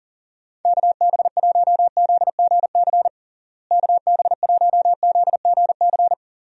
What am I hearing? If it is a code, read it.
Morse code, heard as a keyed tone, about 40 wpm, 700 Hz. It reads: KB1ZGC KB1ZGC
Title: USA Call Signs - 6 Characters - Repeated 2x - 40wpm